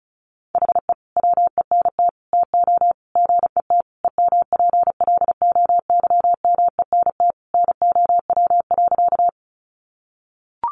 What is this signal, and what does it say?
Morse code, heard as a keyed tone, about 34 wpm, 700 Hz. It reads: HE WENT TO GET EMPLOYMENT NOW.
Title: Sentences from Top 1500 Words 35wpm